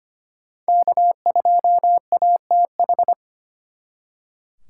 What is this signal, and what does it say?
Morse code, heard as a keyed tone, about 33 wpm, 700 Hz. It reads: K2ATH